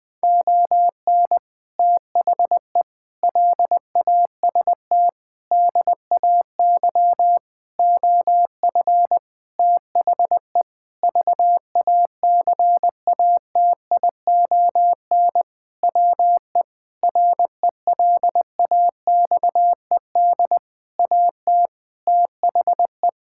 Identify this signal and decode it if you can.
Morse code, heard as a keyed tone, about 20 wpm, 700 Hz. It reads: ON THE LAST DAY OF THE VACATION WE RELAXED AT THE
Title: Sentences with 6-Word Phrases - 20wpm